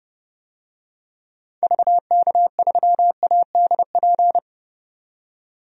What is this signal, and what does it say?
Morse code, heard as a keyed tone, about 32 wpm, 700 Hz. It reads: VK3ADP
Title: DXCC-100 Call Signs - 6 Characters - 30wpm